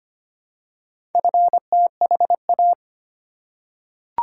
Morse code, heard as a keyed tone, about 25 wpm, 700 Hz. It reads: FTHA
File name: practice / 25wpm / Four Letter-Number Combinations - 25wpm